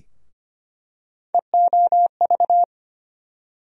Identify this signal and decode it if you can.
Morse code, heard as a keyed tone, about 25 wpm, 700 Hz. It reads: EOV
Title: USA Call Sign Postfixes - 3 Letters 25wpm